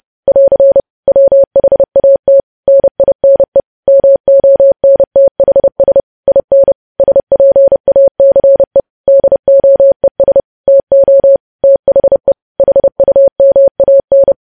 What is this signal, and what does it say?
Morse code, heard as a keyed tone, about 30 wpm, 550 Hz. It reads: <AR> WHAT NINE MONTHS IN SPACE DOES TO THE HUMAN
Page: news headlines in morse code at 30 wpm